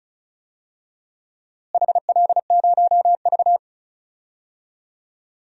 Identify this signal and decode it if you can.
Morse code, heard as a keyed tone, about 36 wpm, 700 Hz. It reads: HL0V
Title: DXCC-100 Call Signs - 4 Characters - Repeated 3x - 35wpm